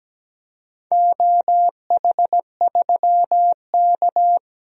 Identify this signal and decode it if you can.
Morse code, heard as a keyed tone, about 17 wpm, 700 Hz. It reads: OH3K
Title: DXCC-100 Call Signs - 4 Characters - 17wpm